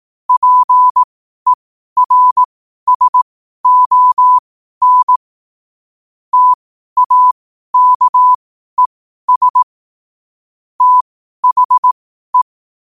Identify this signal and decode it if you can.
Morse code, heard as a keyed tone, about 18 wpm, spaced as for 12 wpm, 1000 Hz. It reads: PERSON TAKES THE
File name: qotdfiles / 12wpm